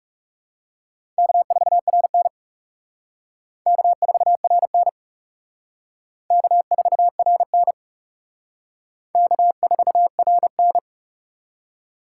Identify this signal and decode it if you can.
Morse code, heard as a keyed tone, about 37 wpm, 700 Hz. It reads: K4RN K4RN K4RN K4RN
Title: Speed Racing - USA 1x2 Call Signs 30wpm